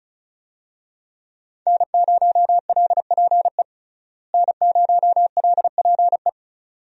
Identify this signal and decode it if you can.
Morse code, heard as a keyed tone, about 36 wpm, 700 Hz. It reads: N0LPE N0LPE
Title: USA Call Signs - 5 Characters - Repeated 2x - 35wpm